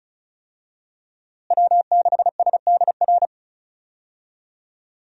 Morse code, heard as a keyed tone, about 34 wpm, 700 Hz. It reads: W6SDR
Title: USA Call Signs - 5 Characters - Repeated 2x - 35wpm